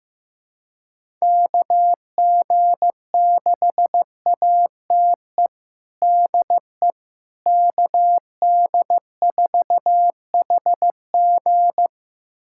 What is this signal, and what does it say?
Morse code, heard as a keyed tone, about 15 wpm, 700 Hz. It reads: KG6ATE DE KD4HG
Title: USA Callsign DE Callsign 15wpm